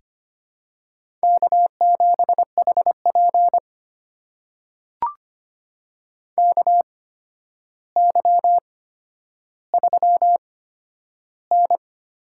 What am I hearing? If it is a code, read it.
Morse code, heard as a keyed tone, about 25 wpm, spaced as for 11 wpm, 700 Hz. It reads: K7HP KY3N
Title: USA Call Signs - 4 Characters - 8x Character Spacing - 25wpm